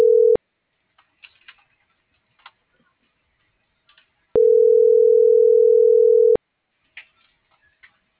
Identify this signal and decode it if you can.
Ringback tone (North America), 440 + 480 Hz.